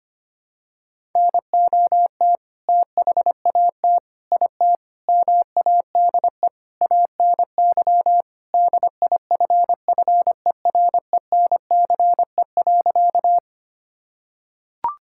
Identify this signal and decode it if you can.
Morse code, heard as a keyed tone, about 25 wpm, 700 Hz. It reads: NOT THAT IT MADE ANY DIFFERENCE.
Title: Sentences from Top 1000 Words - Review Part 5 - 25wpm